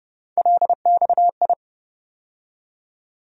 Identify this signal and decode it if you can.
Morse code, heard as a keyed tone, about 30 wpm, 700 Hz. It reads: LXI
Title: USA Call Sign Postfixes - 3 Letters 30wpm